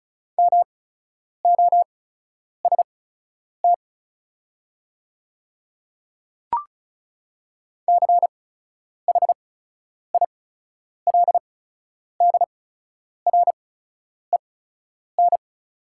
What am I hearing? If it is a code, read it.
Morse code, heard as a keyed tone, about 36 wpm, spaced as for 10 wpm, 700 Hz. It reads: MOST CHILDREN
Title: 8x Character Spacing - Top 300 Words - 35wpm